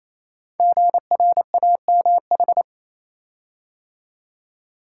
Morse code, heard as a keyed tone, about 27 wpm, 700 Hz. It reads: GRAMH
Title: Code Groups - 5 Letters - 28wpm